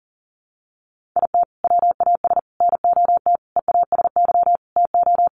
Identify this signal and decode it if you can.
Morse code, heard as a keyed tone, about 40 wpm, 700 Hz. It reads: IT WAS NOT EASY TO
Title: Sentences from Top 1500 Words 40wpm